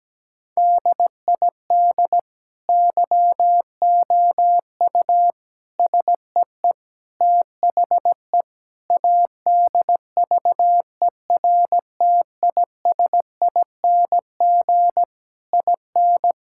Morse code, heard as a keyed tone, about 17 wpm, 700 Hz. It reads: DID YOU SEE THE ADVERTISING IN
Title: Sentences from Top 1000 Words - Review Part 6 - 17wpm